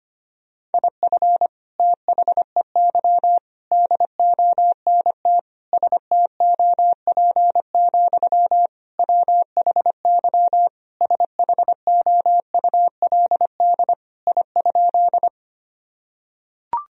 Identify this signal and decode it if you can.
Morse code, heard as a keyed tone, about 25 wpm, 700 Hz. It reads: IF THEY DONT STOP, WHY SHOULD I?
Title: Sentences from Top 1000 Words - Review Part 1 - 25wpm